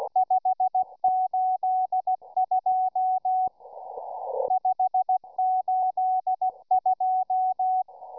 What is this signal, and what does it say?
Morse code, heard as a keyed tone, about 16 wpm, 750 Hz. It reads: 582 582